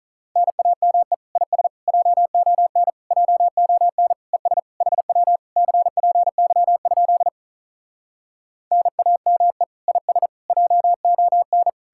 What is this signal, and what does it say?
Morse code, heard as a keyed tone, about 47 wpm, 700 Hz. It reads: NAME IS JON JON ES HW CPY? NAME IS JON